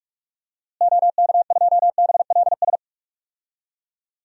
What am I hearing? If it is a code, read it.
Morse code, heard as a keyed tone, about 47 wpm, 700 Hz. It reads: OK2BLS
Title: DXCC-100 Call Signs - 6 Characters - Repeated 2x - 45wpm